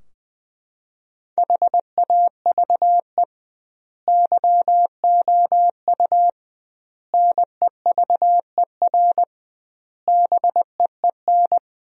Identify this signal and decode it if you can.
Morse code, heard as a keyed tone, about 21 wpm, 700 Hz. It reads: HAVE YOU NEVER BEEN
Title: 2x Word-Spacing - Sentences from Top 300 Words - 20wpm